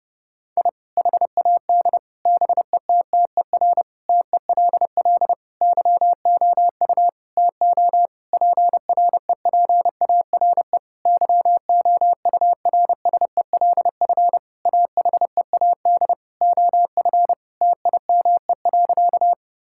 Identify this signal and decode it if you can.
Morse code, heard as a keyed tone, about 32 wpm, 700 Hz. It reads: I HAD BETTER TELL YOU TO PREPARE YOURSELF AHEAD OF TIME.
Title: Sentences from Top 1000 Words - Review Part 6 - 30wpm